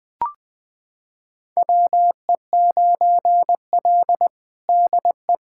Morse code, heard as a keyed tone, about 20 wpm, 700 Hz. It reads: WE9L DE
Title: USA Callsign DE Callsign - 4 Characters - 20wpm